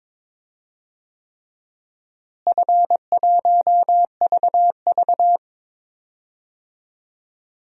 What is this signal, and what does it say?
Morse code, heard as a keyed tone, about 22 wpm, 700 Hz. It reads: F1VV
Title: DXCC-100 Call Signs - 4 Characters - Repeated 3x - 22wpm